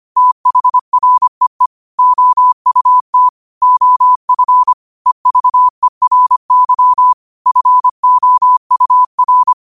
Morse code, heard as a keyed tone, about 25 wpm, 1000 Hz. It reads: THREE OUT OF EVERY FOUR
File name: qotdfiles / 25wpm